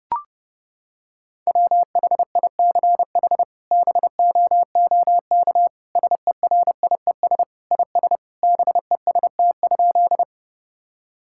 Morse code, heard as a keyed tone, about 30 wpm, 700 Hz. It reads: WHICH BOOK SERIES IS BEST?
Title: Sentences from Top 1000 Words - Review Part 5 - 30wpm